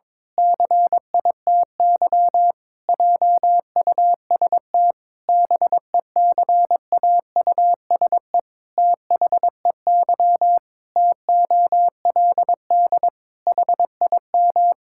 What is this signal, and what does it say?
Morse code, heard as a keyed tone, about 22 wpm, 700 Hz. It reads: CITY JUST BECAUSE THEY TOLD HIM
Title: Pangram Challenge v2 - 22wpm